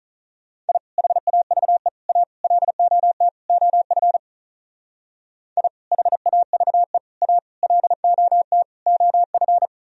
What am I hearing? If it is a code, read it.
Morse code, heard as a keyed tone, about 44 wpm, 700 Hz. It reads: I HAVE A LOT OF I HAVE A LOT OF